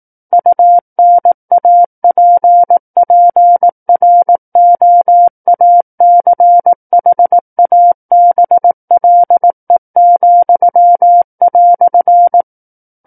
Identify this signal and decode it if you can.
Morse code, heard as a keyed tone, about 18 wpm, 700 Hz. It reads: UNAPPROACHABLE,"
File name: 18 WPM